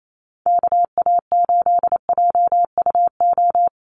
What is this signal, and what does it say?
Morse code, heard as a keyed tone, about 28 wpm, 700 Hz. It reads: KA8JUO
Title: USA Call Signs Repeated 2x 28wpm